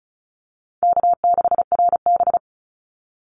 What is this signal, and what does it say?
Morse code, heard as a keyed tone, about 34 wpm, 700 Hz. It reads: K6RB
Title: Easy Call Signs Repeated 2x - 35wpm